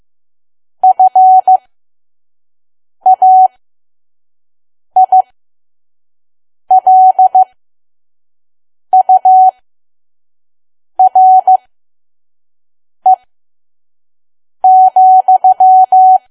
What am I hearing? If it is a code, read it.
Morse code, heard as a keyed tone, about 15 wpm, spaced as for 5 wpm, 750 Hz. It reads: FAILURE,
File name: Archive / 5 WPM files / PM